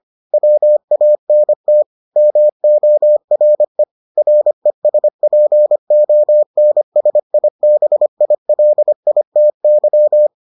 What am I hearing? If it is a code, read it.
Morse code, heard as a keyed tone, about 25 wpm, 600 Hz. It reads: WANT MORE RESPONSIBILITY